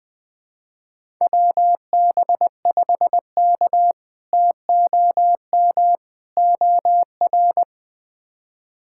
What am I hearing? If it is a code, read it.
Morse code, heard as a keyed tone, about 20 wpm, 700 Hz. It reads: WB5K TOM OR
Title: K1USN SST QSO Practice - 20wpm